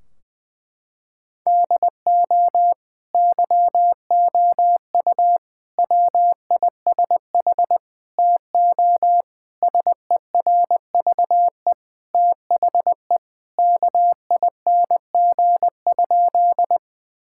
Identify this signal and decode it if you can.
Morse code, heard as a keyed tone, about 21 wpm, 700 Hz. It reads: DO YOU WISH TO SERVE THE KING?
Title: Sentences from Top 1600 Words 20wpm